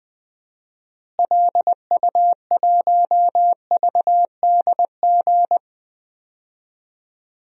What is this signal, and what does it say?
Morse code, heard as a keyed tone, about 20 wpm, 700 Hz. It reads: LU1VDG
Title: DXCC-100 Call Signs - 6 Characters - Repeated 2x - 20wpm